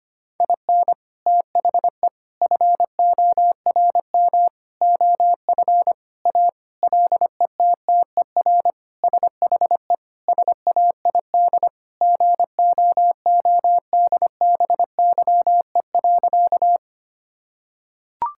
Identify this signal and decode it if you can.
Morse code, heard as a keyed tone, about 25 wpm, 700 Hz. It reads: IN THE FORM OF A LETTER SHE SAID GOODBYE.
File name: practice / 25wpm / Sentences with 4-Word Phrases - 25wpm